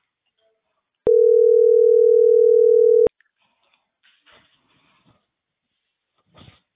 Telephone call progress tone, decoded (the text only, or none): ringback tone